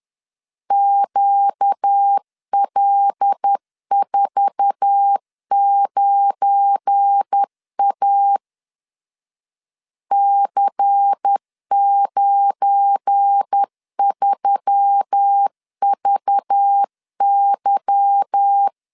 Morse code, heard as a keyed tone, about 10 wpm, 800 Hz. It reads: QL49A C93VY
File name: LEZIONI / 10 SIGNS / 10C DOT 10wpm 2940